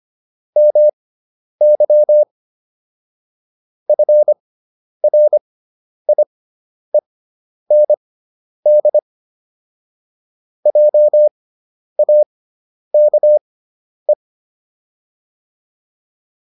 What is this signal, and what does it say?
Morse code, heard as a keyed tone, about 25 wpm, spaced as for 10 wpm, 600 Hz. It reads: MY FRIEND JAKE